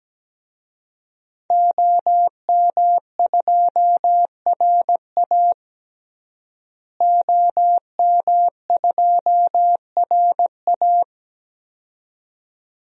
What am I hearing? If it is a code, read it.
Morse code, heard as a keyed tone, about 17 wpm, 700 Hz. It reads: OM2RA OM2RA
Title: DXCC-100 Call Signs - 5 Characters - Repeated 2x - 17wpm